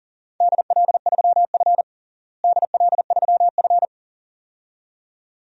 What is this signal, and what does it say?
Morse code, heard as a keyed tone, about 40 wpm, 700 Hz. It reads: DL3F DL3F
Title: DXCC-100 Call Signs - 4 Characters - Repeated 2x - 40wpm